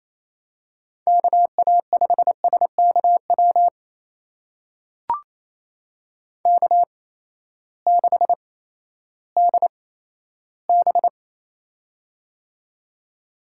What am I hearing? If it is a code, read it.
Morse code, heard as a keyed tone, about 28 wpm, spaced as for 14 wpm, 700 Hz. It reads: KA5SKW K6DB
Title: USA Call Signs - 8x Character Spacing - 28wpm